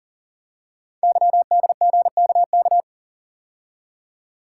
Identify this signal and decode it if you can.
Morse code, heard as a keyed tone, about 44 wpm, 700 Hz. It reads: YDGKK
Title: Code Groups - 5 Letters - 40wpm